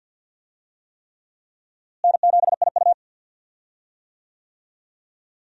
Morse code, heard as a keyed tone, about 52 wpm, 700 Hz. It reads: N7IU
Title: USA Call Signs - 4 Characters - Repeated 3x - 50wpm